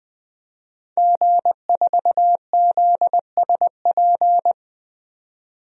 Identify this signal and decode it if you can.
Morse code, heard as a keyed tone, about 20 wpm, 700 Hz. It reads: G4ZSP
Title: DXCC-100 Call Signs - 5 Characters - 20wpm